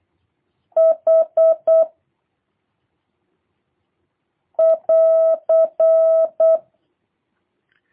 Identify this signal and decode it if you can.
Morse code, heard as a keyed tone, about 8 wpm, 650 Hz. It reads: H <AR>
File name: jh8cbh / 2023cw / review1-2